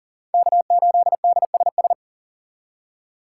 Morse code, heard as a keyed tone, about 40 wpm, 700 Hz. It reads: K8DSS
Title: K1USN SST QSO Practice - 40wpm